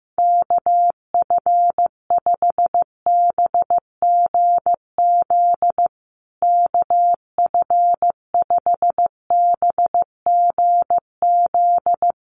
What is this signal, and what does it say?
Morse code, heard as a keyed tone, about 15 wpm, 700 Hz. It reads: KF5BGZ KF5BGZ
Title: USA Call Signs Repeated 3x 15wpm